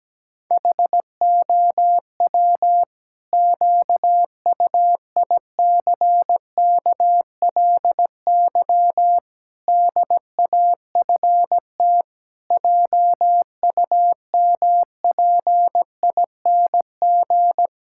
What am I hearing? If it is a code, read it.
Morse code, heard as a keyed tone, about 17 wpm, 700 Hz. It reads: HOW QUICKLY DAFT JUMPING
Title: Pangram Challenge v2 - 17wpm